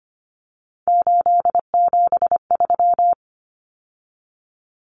Morse code, heard as a keyed tone, about 25 wpm, 700 Hz. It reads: 873
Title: Numbers - 3-Digit 25wpm